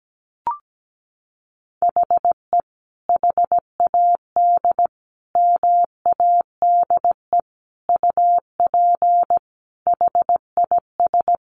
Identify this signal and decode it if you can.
Morse code, heard as a keyed tone, about 17 wpm, 700 Hz. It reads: HE HAD MADE UP HIS
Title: Sentences from Top 500 Words 17wpm